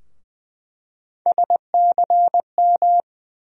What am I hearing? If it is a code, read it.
Morse code, heard as a keyed tone, about 20 wpm, 700 Hz. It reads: SCM
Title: USA Call Sign Postfixes - 3 Letters 20wpm